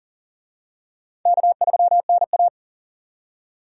Morse code, heard as a keyed tone, about 40 wpm, 700 Hz. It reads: K3NA
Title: USA Call Signs - 4 Characters - 40wpm